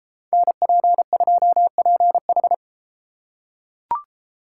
Morse code, heard as a keyed tone, about 32 wpm, 700 Hz. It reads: NP2PH